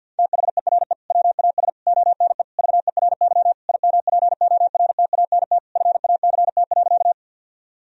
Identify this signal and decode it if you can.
Morse code, heard as a keyed tone, about 50 wpm, 700 Hz. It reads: THERE WAS ONE VERY IMPORTANT FACT.
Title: Sentences from Top 1000 Words - Review Part 3 - 50wpm